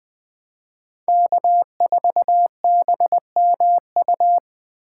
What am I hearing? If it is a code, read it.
Morse code, heard as a keyed tone, about 20 wpm, 700 Hz. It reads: K4BMU